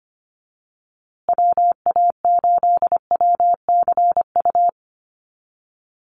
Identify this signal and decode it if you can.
Morse code, heard as a keyed tone, about 25 wpm, 700 Hz. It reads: WA8WCU
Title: USA Call Signs Repeated 3x 25wpm